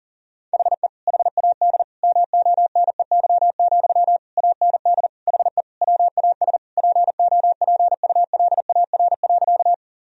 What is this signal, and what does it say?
Morse code, heard as a keyed tone, about 37 wpm, 700 Hz. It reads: HE HAD MONEY, AND HE WAS POPULAR.
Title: Sentences from Top 1000 Words - Review Part 3 - 40wpm